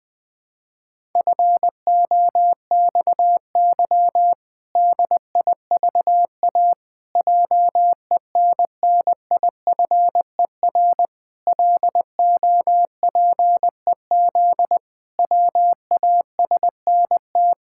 Morse code, heard as a keyed tone, about 20 wpm, 700 Hz. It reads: FOXY DIVA JENNIFER LOPEZ WASNT
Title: Pangram Challenge v2 - 20wpm